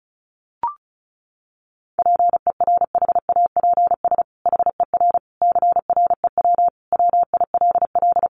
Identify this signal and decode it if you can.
Morse code, heard as a keyed tone, about 34 wpm, 700 Hz. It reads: PERHAPS HER CREW WILL
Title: Sentences from Top 1500 Words 35wpm